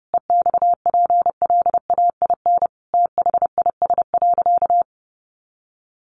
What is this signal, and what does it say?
Morse code, heard as a keyed tone, about 30 wpm, 700 Hz. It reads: EXPLAIN THIS.
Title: Sentences from Top 1100 Words 30wpm